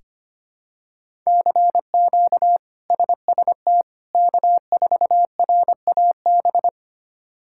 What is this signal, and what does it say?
Morse code, heard as a keyed tone, about 25 wpm, 700 Hz. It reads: CQ SST K4RAB
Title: K1USN SST QSO Practice - 25wpm